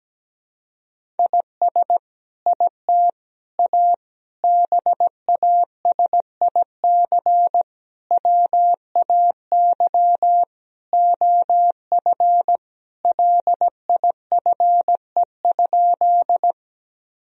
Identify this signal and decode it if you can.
Morse code, heard as a keyed tone, about 17 wpm, 700 Hz. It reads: IS IT A BASIC WAY OF LIFE?
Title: Sentences from Top 1000 Words - Review Part 3 - 17wpm